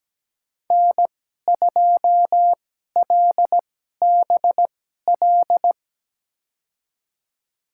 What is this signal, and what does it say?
Morse code, heard as a keyed tone, about 17 wpm, spaced as for 12 wpm, 700 Hz. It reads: N2LBL
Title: USA Call Signs - 5 Characters - 2x Character Spacing - 17wpm